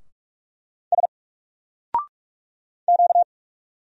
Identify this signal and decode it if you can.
Morse code, heard as a keyed tone, about 44 wpm, 700 Hz. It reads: S Q
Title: Single Character (Letter:Number:Punctuation:Prosign) 45wpm